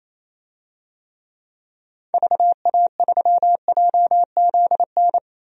Morse code, heard as a keyed tone, about 28 wpm, 700 Hz. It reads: VA3JZN
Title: DXCC-100 Call Signs - 6 Characters - 28wpm